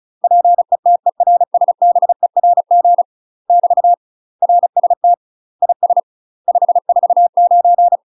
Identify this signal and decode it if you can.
Morse code, heard as a keyed tone, about 33 wpm, 700 Hz. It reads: PETERSBERG <BT> RST IS 549
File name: UK MASTER QSOs TRAINING / 35WPM / UK MASTER QSOs 0026